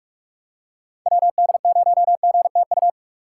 Morse code, heard as a keyed tone, about 46 wpm, 700 Hz. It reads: WD0GTU
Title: USA Call Signs - 6x Character Spacing - 45wpm